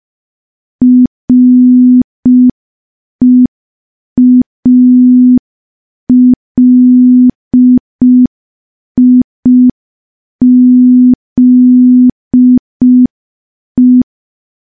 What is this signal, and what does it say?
Morse code, heard as a keyed tone, about 5 wpm, 250 Hz. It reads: REALIZE
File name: morse / morse-training-05WPM